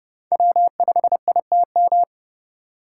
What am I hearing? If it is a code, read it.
Morse code, heard as a keyed tone, about 28 wpm, 700 Hz. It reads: W5ITM